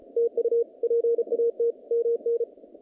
Morse code, heard as a keyed tone, about 34 wpm, 470 Hz. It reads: TU PAT MN
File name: contests / naqp-cw-k5tr-op-k5pi- -2200